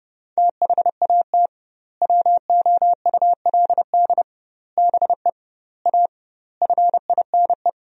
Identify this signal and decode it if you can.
Morse code, heard as a keyed tone, about 30 wpm, 700 Hz. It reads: THAT WOULD BE A FINE
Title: 2x Word-Spacing - Sentences from Top 500 Words - 30wpm